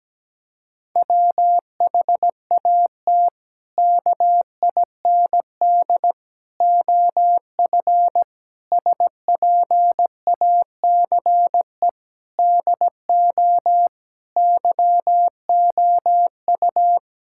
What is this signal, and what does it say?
Morse code, heard as a keyed tone, about 17 wpm, 700 Hz. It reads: WHAT KIND OF SPACE DO YOU
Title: Sentences from Top 1000 Words - Review Part 5 - 17wpm